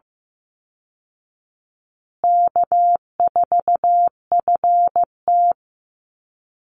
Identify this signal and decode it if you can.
Morse code, heard as a keyed tone, about 15 wpm, 700 Hz. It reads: K4FT
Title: Easy Call Signs Repeated 3x - 15wpm